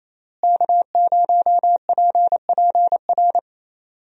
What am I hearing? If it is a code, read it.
Morse code, heard as a keyed tone, about 27 wpm, 700 Hz. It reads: K0PPR
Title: USA Call Signs - 5 Characters - Repeated 3x - 28wpm